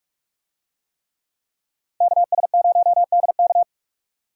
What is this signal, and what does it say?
Morse code, heard as a keyed tone, about 46 wpm, 700 Hz. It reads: KS0DK